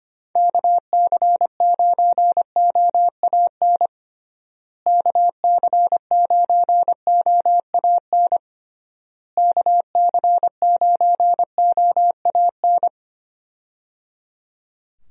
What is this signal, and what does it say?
Morse code, heard as a keyed tone, about 25 wpm, 700 Hz. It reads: KC9OAN KC9OAN KC9OAN
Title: USA Call Signs - 6 Characters - Repeated 3x - 25wpm